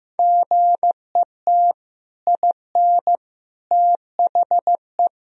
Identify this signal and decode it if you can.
Morse code, heard as a keyed tone, about 16 wpm, 700 Hz. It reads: GET IN THE
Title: Sentences from Top 1000 Words - Review Part 3 - 15wpm